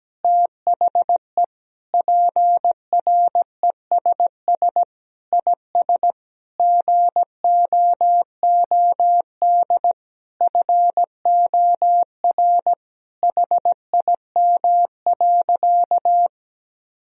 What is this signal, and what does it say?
Morse code, heard as a keyed tone, about 17 wpm, 700 Hz. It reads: THE PRESS IS GOOD FOR HIM.
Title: Sentences from Top 1000 Words - Review Part 1 - 17wpm